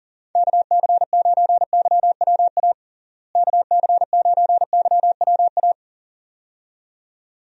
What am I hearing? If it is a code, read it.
Morse code, heard as a keyed tone, about 40 wpm, 700 Hz. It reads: KC9YWA KC9YWA
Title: USA Call Signs - 6 Characters - Repeated 2x - 40wpm